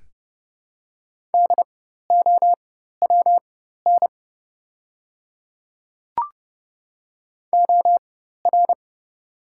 Morse code, heard as a keyed tone, about 30 wpm, spaced as for 14 wpm, 700 Hz. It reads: DOWN OR
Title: 4x Character Spacing - Top 100 Words - 30wpm